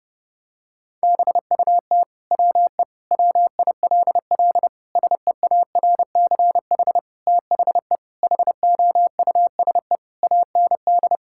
Morse code, heard as a keyed tone, about 30 wpm, 700 Hz. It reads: BUT WE WILL SEARCH THE HOUSE AND
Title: Sentences from Top 1000 Words - Review Part 5 - 30wpm